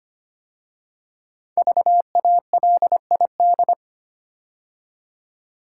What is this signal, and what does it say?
Morse code, heard as a keyed tone, about 25 wpm, 700 Hz. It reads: VALID
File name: 5 Letter Words - First 3 Letters Given - 25wpm